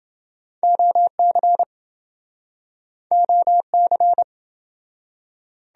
Morse code, heard as a keyed tone, about 26 wpm, 700 Hz. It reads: OC OC